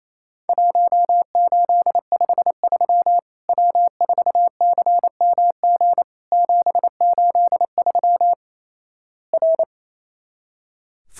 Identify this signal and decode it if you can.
Morse code, heard as a keyed tone, about 27 wpm, 700 Hz. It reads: 1853 W4CMG 783 R
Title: Worked All Europe DX Contest - QTC Exchange - 28wpm